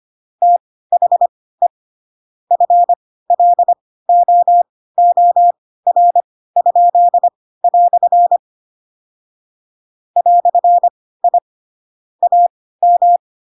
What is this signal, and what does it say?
Morse code, heard as a keyed tone, about 25 wpm, spaced as for 16 wpm, 700 Hz. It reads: THE FLOOR?" "I AM
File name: Pinocchio / Pinocchio ch2 10 25 ik1hge